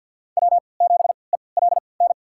Morse code, heard as a keyed tone, about 52 wpm, spaced as for 29 wpm, 700 Hz. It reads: W7ELN
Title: USA Call Signs - 3x Character Spacing - 50wpm